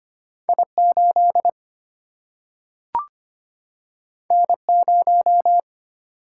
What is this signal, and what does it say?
Morse code, heard as a keyed tone, about 25 wpm, 700 Hz. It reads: I8 N0
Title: Single-Letter Single-Number 25wpm